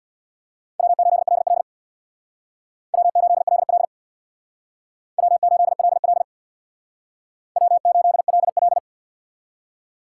Morse code, heard as a keyed tone, about 64 wpm, 700 Hz. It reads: W8LL W8LL W8LL W8LL